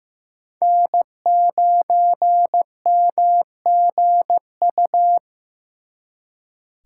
Morse code, heard as a keyed tone, about 18 wpm, 700 Hz. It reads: N9MGU
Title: Speed Racing - USA 1x3 Call Signs 15wpm